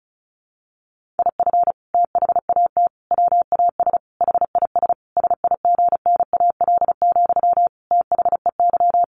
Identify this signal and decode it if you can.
Morse code, heard as a keyed tone, about 34 wpm, 700 Hz. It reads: IF THAT WAS HIS SIGNAL, THEY
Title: Sentences from Top 1400 Words 35wpm